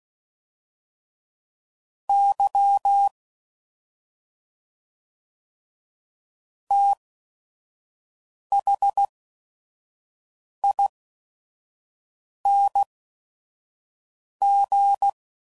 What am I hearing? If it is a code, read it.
Morse code, heard as a keyed tone, about 16 wpm, spaced as for 5 wpm, 800 Hz. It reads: Y THING
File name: Suffixes / Special ING's / SlowToFaster ING's